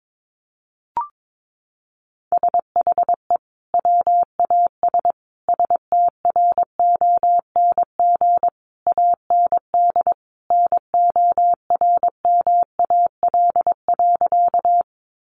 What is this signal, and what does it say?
Morse code, heard as a keyed tone, about 22 wpm, 700 Hz. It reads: SHE WAS STRONG AND NORMAL.
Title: Sentences from Top 1200 Words 22wpm